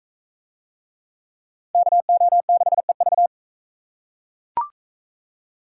Morse code, heard as a keyed tone, about 37 wpm, 700 Hz. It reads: KO6EV